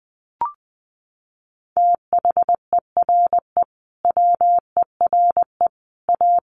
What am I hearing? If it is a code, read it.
Morse code, heard as a keyed tone, about 20 wpm, 700 Hz. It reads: THERE WERE A
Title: Sets of 5 Words Encore2 20wpm